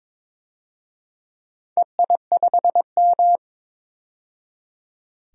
Morse code, heard as a keyed tone, about 22 wpm, 700 Hz. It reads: EI5M
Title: DXCC-100 Call Signs - 4 Characters - 22wpm